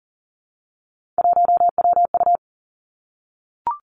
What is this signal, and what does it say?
Morse code, heard as a keyed tone, about 38 wpm, 700 Hz. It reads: 1WU
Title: Three Challenging Letter-Number Combinations 40wpm